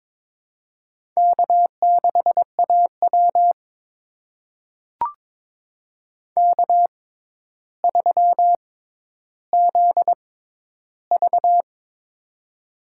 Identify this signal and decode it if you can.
Morse code, heard as a keyed tone, about 22 wpm, spaced as for 11 wpm, 700 Hz. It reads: K6AW K3ZV
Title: USA Call Signs - 4 Characters - 6x Character Spacing - 22wpm